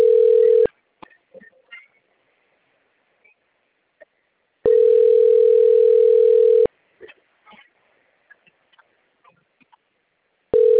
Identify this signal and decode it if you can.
Ringback tone (North America), 440 + 480 Hz.
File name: rg-1600-unknown- - - .74920